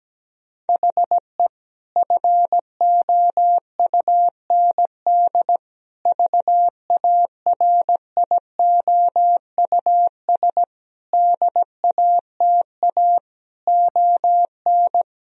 Morse code, heard as a keyed tone, about 17 wpm, 700 Hz. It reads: HE FOUND VARIOUS DATA ON
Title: Sentences from Top 1000 Words - Review Part 4 - 17wpm